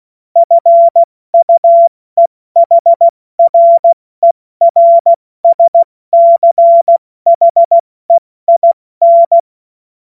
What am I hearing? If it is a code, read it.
Morse code, heard as a keyed tone, about 16 wpm, spaced as for 14 wpm, 680 Hz. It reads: FUEHRERSCHEIN
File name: Morsekurs / Witze / Witz 03. August 2021, 70 F80